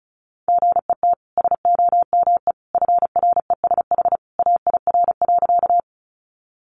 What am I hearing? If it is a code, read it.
Morse code, heard as a keyed tone, about 34 wpm, 700 Hz. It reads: GET SOME FRESH AIR.
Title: Sentences from Top 1200 Words 35wpm